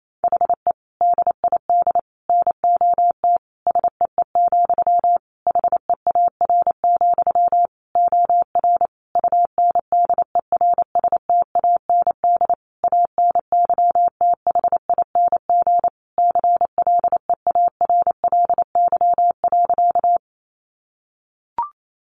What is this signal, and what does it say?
Morse code, heard as a keyed tone, about 28 wpm, 700 Hz. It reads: HE DID NOT SEE, HEAR, OR UNDERSTAND ANYTHING CLEARLY.
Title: Sentences from Top 1300 Words 28wpm